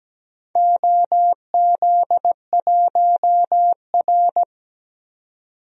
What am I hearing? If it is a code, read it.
Morse code, heard as a keyed tone, about 17 wpm, 700 Hz. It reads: OZ1R
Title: DXCC-100 Call Signs - 4 Characters - 17wpm